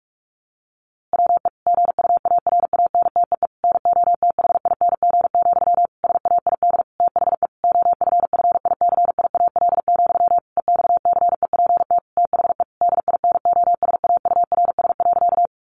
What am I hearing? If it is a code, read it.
Morse code, heard as a keyed tone, about 44 wpm, 700 Hz. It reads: WE GUARANTEE NOTHING, SAID THE OFFICIAL, EXCEPT THE DINOSAURS.
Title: A Sound Of Thunder 45wpm